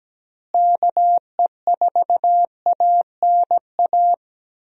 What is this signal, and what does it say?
Morse code, heard as a keyed tone, about 17 wpm, 700 Hz. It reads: KE4ANA
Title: USA Call Signs - 6 Characters - Repeated 3x - 17wpm